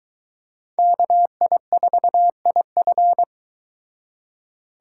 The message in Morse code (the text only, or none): KI4IF